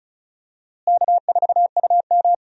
Morse code, heard as a keyed tone, about 36 wpm, 700 Hz. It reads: K4UM
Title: USA Call Signs - 4 Characters - Repeated 3x - 35wpm